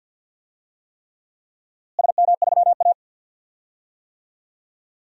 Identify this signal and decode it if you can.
Morse code, heard as a keyed tone, about 53 wpm, 700 Hz. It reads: SM3A
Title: DXCC-100 Call Signs - 4 Characters - Repeated 3x - 50wpm